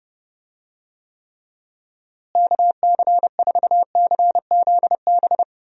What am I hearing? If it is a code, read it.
Morse code, heard as a keyed tone, about 31 wpm, 700 Hz. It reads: KC4CZB
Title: USA Call Signs - 6 Characters - Repeated 2x - 30wpm